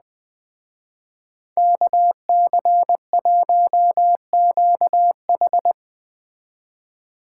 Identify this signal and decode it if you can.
Morse code, heard as a keyed tone, about 20 wpm, 700 Hz. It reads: KC1QH